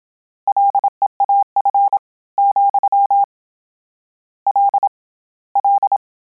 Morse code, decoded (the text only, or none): LEAF , L L